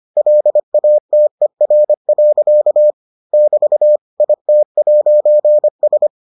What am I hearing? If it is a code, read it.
Morse code, heard as a keyed tone, about 25 wpm, 600 Hz. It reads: LATER. <BT> IT'S